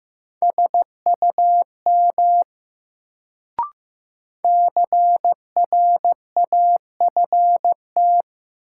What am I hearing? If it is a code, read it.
Morse code, heard as a keyed tone, about 15 wpm, 700 Hz. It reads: SUM CRAFT
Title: Sentences from Top 1000 Words - Review Part 6 - New Word Drill - 15wpm